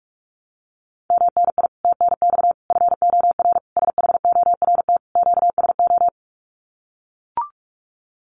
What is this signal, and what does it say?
Morse code, heard as a keyed tone, about 44 wpm, 700 Hz. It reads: MNI TNX FOR SHORT QSO
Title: Common QSO Phrases 45wpm